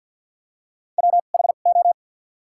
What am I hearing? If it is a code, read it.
Morse code, heard as a keyed tone, about 54 wpm, spaced as for 37 wpm, 700 Hz. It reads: WHO
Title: 2x Character Spacing - Top 100 Words - 50wpm